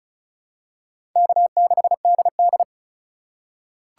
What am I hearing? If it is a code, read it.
Morse code, heard as a keyed tone, about 34 wpm, 700 Hz. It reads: K6DD
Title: USA Call Signs - 4 Characters - 4x Character Spacing - 35wpm